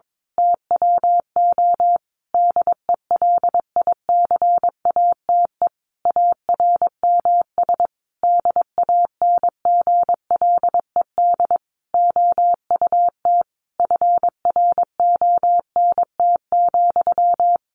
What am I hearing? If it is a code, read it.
Morse code, heard as a keyed tone, about 22 wpm, 700 Hz. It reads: TWO DELICATE ARMS DANGLED OUT FRONT,
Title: A Sound Of Thunder 22wpm